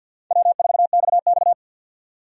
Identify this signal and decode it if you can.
Morse code, heard as a keyed tone, about 48 wpm, 700 Hz. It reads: W4XX
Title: USA Call Signs - 4 Characters - 50wpm